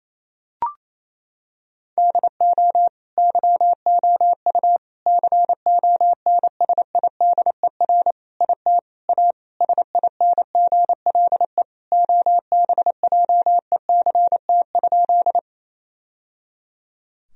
Morse code, heard as a keyed tone, about 28 wpm, 700 Hz. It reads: DO YOU CONSIDER IT A SINGLE OBJECT?
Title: Sentences from Top 1000 Words - Review Part 4 - 28wpm